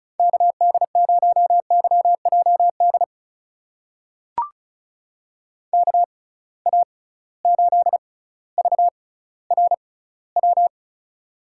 Morse code, heard as a keyed tone, about 34 wpm, spaced as for 18 wpm, 700 Hz. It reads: KD0YJD KA8VRW
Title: USA Call Signs - 6x Character Spacing - 35wpm